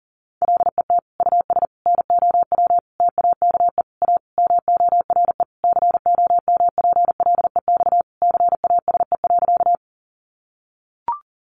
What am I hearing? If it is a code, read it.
Morse code, heard as a keyed tone, about 40 wpm, 700 Hz. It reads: LET US NOW TAKE A MORE COMPLEX CASE.
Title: Sentences from Top 900 Words 40wpm